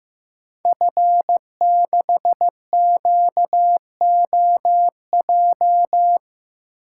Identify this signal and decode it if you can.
Morse code, heard as a keyed tone, about 15 wpm, 700 Hz. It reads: F6QOJ
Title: Code Groups - 5 Letter:Numbers - 15wpm